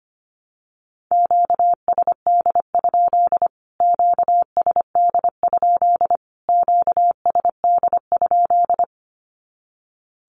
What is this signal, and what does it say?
Morse code, heard as a keyed tone, about 25 wpm, 700 Hz. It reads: QSD? QSD? QSD?
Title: Q-Codes Repeated 3x 25wpm